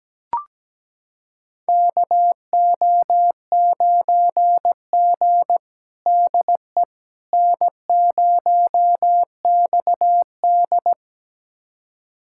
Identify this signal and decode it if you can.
Morse code, heard as a keyed tone, about 17 wpm, 700 Hz. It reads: KO9G DE N0XD
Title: USA Callsign DE Callsign - 4 Characters - 17wpm